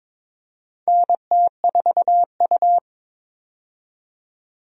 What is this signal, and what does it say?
Morse code, heard as a keyed tone, about 21 wpm, 700 Hz. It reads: NT4U
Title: USA Call Signs - 4 Characters - 22wpm